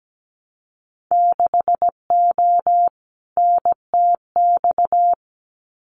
Morse code, heard as a keyed tone, about 17 wpm, 700 Hz. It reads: 6O NTX
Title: Winter Field Day Exchanges Encore 17wpm